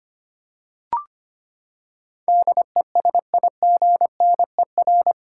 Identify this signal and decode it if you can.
Morse code, heard as a keyed tone, about 25 wpm, 700 Hz. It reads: DESIGNER
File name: Eight Letter Words - 25wpm